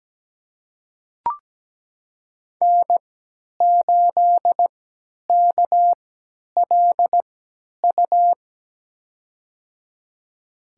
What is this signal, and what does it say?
Morse code, heard as a keyed tone, about 17 wpm, spaced as for 10 wpm, 700 Hz. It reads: N8KLU
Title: USA Call Signs - 5 Characters - 3x Character Spacing - 17wpm